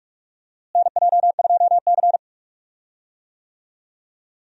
Morse code, heard as a keyed tone, about 46 wpm, 700 Hz. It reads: NJ2C
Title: USA Call Signs - 4 Characters - Repeated 3x - 45wpm